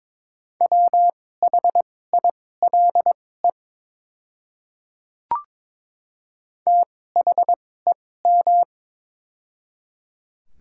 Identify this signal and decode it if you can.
Morse code, heard as a keyed tone, about 22 wpm, spaced as for 16 wpm, 700 Hz. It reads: WHILE THEM
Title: 2x Character Spacing - Top 300 Words - 22wpm